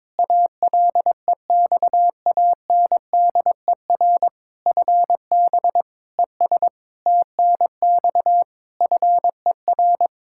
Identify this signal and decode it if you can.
Morse code, heard as a keyed tone, about 21 wpm, 700 Hz. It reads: ALEXANDER FB ES TNX FER
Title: K1USN SST QSO Practice 22wpm